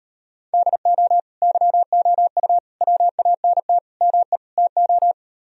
Morse code, heard as a keyed tone, about 39 wpm, 700 Hz. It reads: DO YOU WANT ME TO